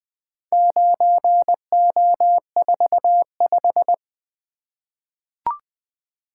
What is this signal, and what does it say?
Morse code, heard as a keyed tone, about 20 wpm, 700 Hz. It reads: 9O45
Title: Four Letter-Number Combinations - 20wpm